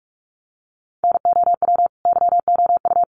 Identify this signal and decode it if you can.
Morse code, heard as a keyed tone, about 44 wpm, 700 Hz. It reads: NOW YOU
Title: Sentences from Top 1500 Words 45wpm